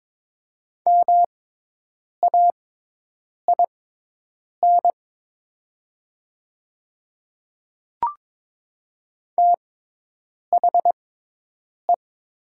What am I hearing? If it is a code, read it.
Morse code, heard as a keyed tone, about 22 wpm, spaced as for 8 wpm, 700 Hz. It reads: MAIN THE
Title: 6x Character Spacing - Top 300 Words - 22wpm